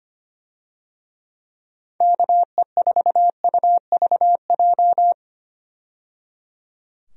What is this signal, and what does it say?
Morse code, heard as a keyed tone, about 25 wpm, 700 Hz. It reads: KE4UVJ
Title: USA Call Signs - 6 Characters - 25wpm